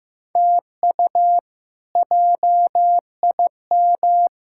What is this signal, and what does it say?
Morse code, heard as a keyed tone, about 14 wpm, 700 Hz. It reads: TU JIM
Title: K1USN SST QSO Practice - 15wpm